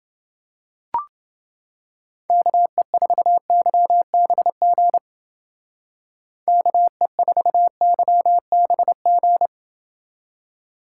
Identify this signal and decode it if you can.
Morse code, heard as a keyed tone, about 28 wpm, 700 Hz. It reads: KE4YBG KE4YBG